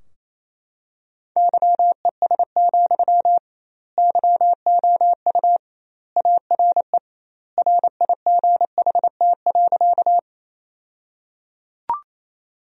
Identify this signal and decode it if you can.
Morse code, heard as a keyed tone, about 28 wpm, 700 Hz. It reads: YES, YOU ARE RIGHT.
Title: 2x Word-Spacing - Sentences from Top 500 Words - 28wpm